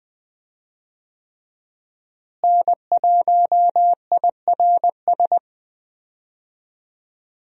Morse code, heard as a keyed tone, about 20 wpm, 700 Hz. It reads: N1IRS
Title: USA Call Signs - 5 Characters - Repeated 3x - 20wpm